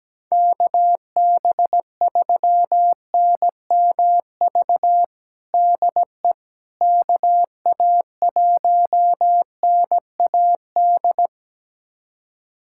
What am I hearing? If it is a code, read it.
Morse code, heard as a keyed tone, about 17 wpm, 700 Hz. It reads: KB3NMV DE KA1NAD
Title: USA Callsign DE Callsign - 6 Characters - 17wpm